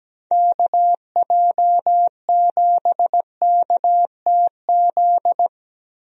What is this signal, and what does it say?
Morse code, heard as a keyed tone, about 17 wpm, 700 Hz. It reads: KJ7KTZ
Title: USA Call Signs - 6 Characters - 17wpm